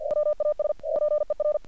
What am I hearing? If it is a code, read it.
Morse code, heard as a keyed tone, about 45 wpm, 600 Hz. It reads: WAS PER
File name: archived / CRSnet